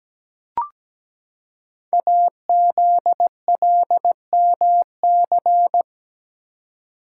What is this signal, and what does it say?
Morse code, heard as a keyed tone, about 17 wpm, 700 Hz. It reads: AZLMC
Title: Code Groups - 5 Letters - 17wpm